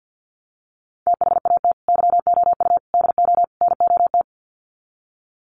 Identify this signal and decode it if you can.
Morse code, heard as a keyed tone, about 50 wpm, 700 Hz. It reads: THAT YOU DO NOT
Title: Sets of 4 Words 50wpm